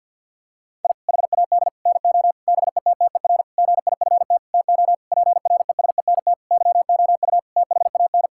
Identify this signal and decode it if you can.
Morse code, heard as a keyed tone, about 50 wpm, 700 Hz. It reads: I HAD NO BETTER GIFT TO PRESENT YOU THAN
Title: Sentences from Top 1000 Words - Review Part 6 - 50wpm